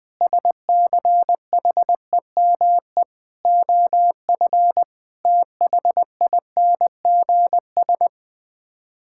Morse code, heard as a keyed tone, about 20 wpm, 700 Hz. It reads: SCHEME OF THINGS
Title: Sets of 6 Words - 20wpm